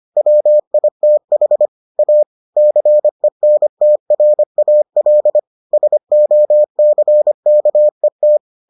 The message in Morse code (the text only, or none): WITH A CENTRAL SOCKET